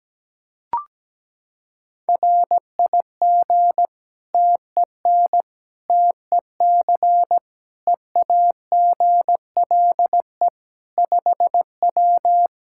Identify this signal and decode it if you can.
Morse code, heard as a keyed tone, about 17 wpm, 700 Hz. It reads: RIG TEN TEC EAGLE 5W
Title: Practice QSO - Exchange 2 - Info - 17wpm